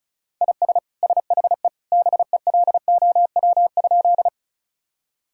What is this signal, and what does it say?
Morse code, heard as a keyed tone, about 34 wpm, 700 Hz. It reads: IS SHE BELOW?
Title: Sentences from Top 1000 Words - Review Part 5 - 35wpm